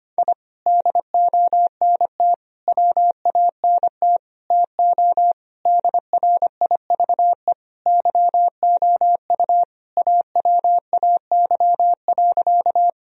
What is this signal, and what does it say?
Morse code, heard as a keyed tone, about 25 wpm, 700 Hz. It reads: I DONT WANT TO DRIVE YOU AWAY.
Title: Sentences from Top 1000 Words - Review Part 2 - 25wpm